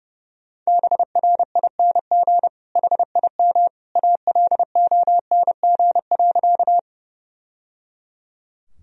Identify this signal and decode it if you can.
Morse code, heard as a keyed tone, about 29 wpm, 700 Hz. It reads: BRING HIM ALONG.
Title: Sentences from Top 1000 Words - Review Part 3 - 30wpm